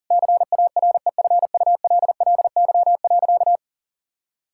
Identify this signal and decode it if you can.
Morse code, heard as a keyed tone, about 40 wpm, 700 Hz. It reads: CAREFULLY.
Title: Sentences from Top 1600 Words 40wpm